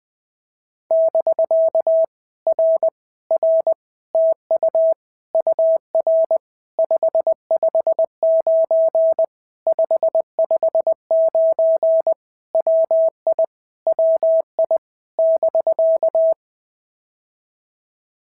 Morse code, heard as a keyed tone, about 19 wpm, 650 Hz. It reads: <BK> R R TU UR 559 559 WI WI <BK>
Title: POTA QSO Practice - 20wpm